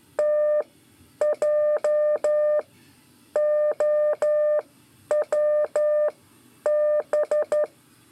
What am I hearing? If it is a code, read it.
Morse code, heard as a keyed tone, about 12 wpm, spaced as for 9 wpm, 600 Hz. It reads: TJOWB
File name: 5 LEZIONE RIPASSO 1